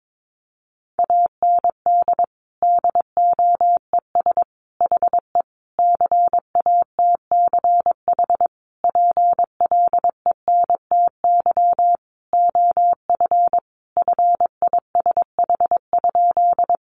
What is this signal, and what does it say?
Morse code, heard as a keyed tone, about 22 wpm, 700 Hz. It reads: AND DOES HE CATCH PLENTY OF FISH?
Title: Sentences from Top 1500 Words 22wpm